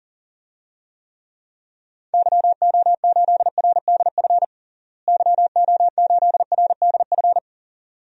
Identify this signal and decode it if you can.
Morse code, heard as a keyed tone, about 40 wpm, 700 Hz. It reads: YO8RDF YO8RDF
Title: DXCC-100 Call Signs - 6 Characters - Repeated 2x - 40wpm